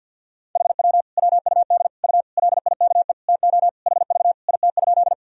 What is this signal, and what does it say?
Morse code, heard as a keyed tone, about 51 wpm, 700 Hz. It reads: HW WUD U LIKE TO HV IT?
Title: Sentences from Top 100 Words and Abbreviations 50wpm